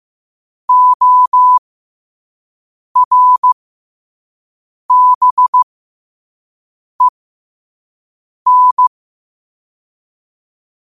Morse code, heard as a keyed tone, about 15 wpm, spaced as for 5 wpm, 1000 Hz. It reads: ORBEN
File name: qotdfiles / 5wpm